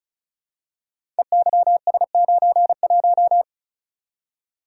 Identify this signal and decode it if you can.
Morse code, heard as a keyed tone, about 36 wpm, 690 Hz. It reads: EYS91
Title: Code Groups - 5 Letter:Numbers - 35wpm